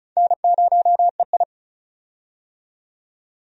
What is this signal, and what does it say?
Morse code, heard as a keyed tone, about 36 wpm, 700 Hz. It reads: N0EI
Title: USA Call Signs - 4 Characters - 35wpm